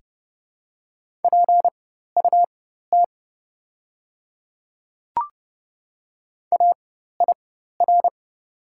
Morse code, heard as a keyed tone, about 32 wpm, spaced as for 14 wpm, 700 Hz. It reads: PUT AIR
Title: 4x Character Spacing - Top 200 Words - 30wpm